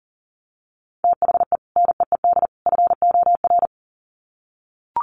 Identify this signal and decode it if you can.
Morse code, heard as a keyed tone, about 40 wpm, 700 Hz. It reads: THE NEED FOR
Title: Sets of 3 Words 40wpm